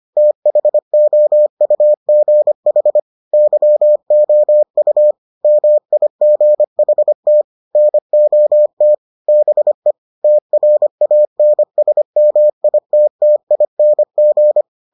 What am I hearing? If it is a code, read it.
Morse code, heard as a keyed tone, about 25 wpm, 600 Hz. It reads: THOUGH YOU MIGHT NOT BE TRANSMITTING